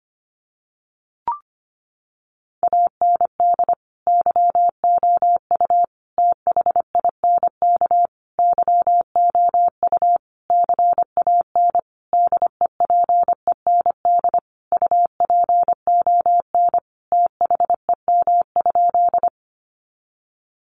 Morse code, heard as a keyed tone, about 25 wpm, 700 Hz. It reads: AND YOU THINK YOU CAN DEPEND UPON THEM?
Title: Sentences from Top 1300 Words 25wpm